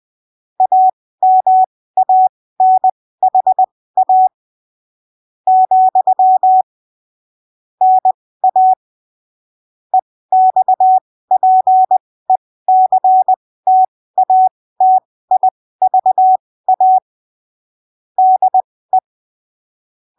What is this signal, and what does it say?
Morse code, heard as a keyed tone, about 20 wpm, spaced as for 15 wpm, 750 Hz. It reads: AMANHA , NA EXPECTATIVA DE
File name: arquivos / texto / texto 19 20ppm